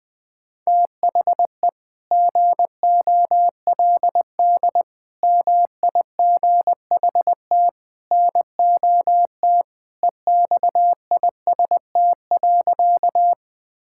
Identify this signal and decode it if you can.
Morse code, heard as a keyed tone, about 20 wpm, 700 Hz. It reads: THE GOLD MIGHT NOT EXIST.
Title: Sentences from Top 1000 Words - Review Part 5 - 20wpm